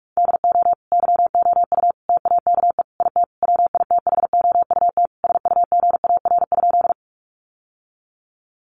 Morse code, heard as a keyed tone, about 44 wpm, 700 Hz. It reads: DO YOU TAKE IT WITHOUT SUGAR?
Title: Sentences from Top 1200 Words 45wpm